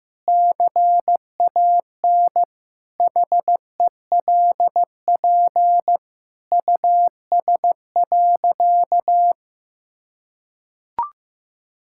Morse code, heard as a keyed tone, about 14 wpm, 700 Hz. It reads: CAN HELP US.
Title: Sentences from Top 1000 Words - Review Part 5 - 15wpm